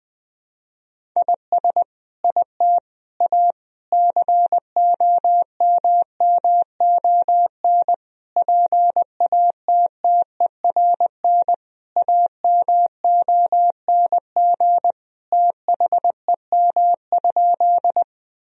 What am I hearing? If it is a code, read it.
Morse code, heard as a keyed tone, about 21 wpm, 700 Hz. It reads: IS IT A COMMON PATTERN AMONG THEM?
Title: Sentences from Top 1000 Words - Review Part 2 - 20wpm